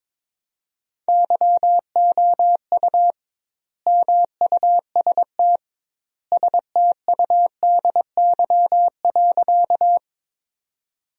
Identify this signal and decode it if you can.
Morse code, heard as a keyed tone, about 22 wpm, 700 Hz. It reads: YOU MUST STUDY.
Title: 2x Word-Spacing - Sentences from Top 300 Words - 22wpm